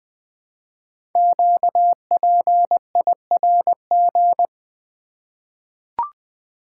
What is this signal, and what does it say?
Morse code, heard as a keyed tone, about 20 wpm, 700 Hz. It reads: QPIRG
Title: Code Groups - 5 Letters - 20wpm